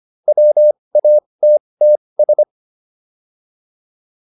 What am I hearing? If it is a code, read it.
Morse code, heard as a keyed tone, about 25 wpm, spaced as for 20 wpm, 600 Hz. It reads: WATTS